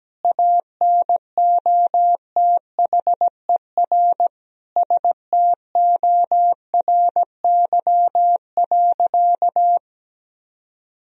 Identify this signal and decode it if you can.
Morse code, heard as a keyed tone, about 17 wpm, 700 Hz. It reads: ANOTHER STORY.
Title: Sentences from Top 1000 Words - Review Part 1 - 17wpm